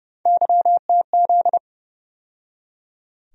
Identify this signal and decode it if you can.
Morse code, heard as a keyed tone, about 30 wpm, 700 Hz. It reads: YTZ
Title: USA Call Sign Postfixes - 3 Letters 30wpm